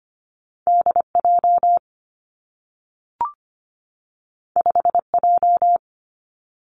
Morse code, heard as a keyed tone, about 25 wpm, 700 Hz. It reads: D J 5 J
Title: Two Challenging Letter-Number Combinations 25wpm